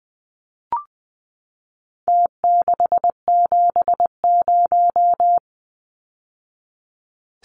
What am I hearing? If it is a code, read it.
Morse code, heard as a keyed tone, about 20 wpm, 700 Hz. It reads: T670
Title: Numbers - Contest Style - 4-Digit 20wpm